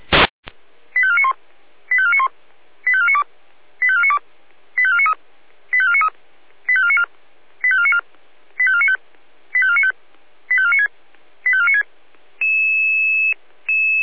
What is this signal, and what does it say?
Five-tone selective call (ZVEI), heard as quick runs of five tones, tones of about 70 ms. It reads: 85491 85491 85492 85492 85493 85493 85494 85494 85495 85495 85496 85496
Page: ZVEI